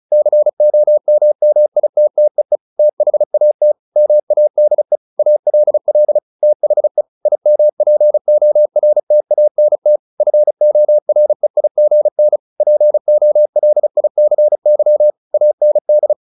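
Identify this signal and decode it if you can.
Morse code, heard as a keyed tone, about 34 wpm, 600 Hz. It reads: COMMITTEE THAT MADE ALL THE IMPORTANT FOREIGN POLICY AND
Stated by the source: Morse practice at 35 wpm from CW med Gnister, sending Ken Follett's novel Never in English.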